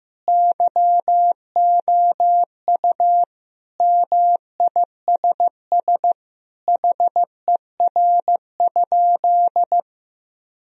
Morse code, heard as a keyed tone, about 16 wpm, 700 Hz. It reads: YOU MISS HER?
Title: Sentences from Top 1000 Words - Review Part 3 - 15wpm